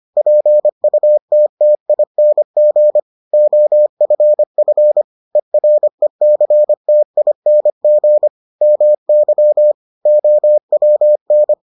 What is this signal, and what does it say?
Morse code, heard as a keyed tone, about 25 wpm, 600 Hz. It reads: PUTTING OFF ERECTING MY OWN